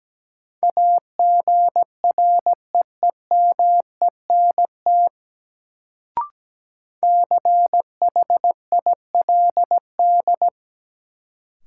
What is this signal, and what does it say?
Morse code, heard as a keyed tone, about 17 wpm, 700 Hz. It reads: AGREEMENT CHILD
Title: Sets of 5 Words Encore - New Word Drill - 17wpm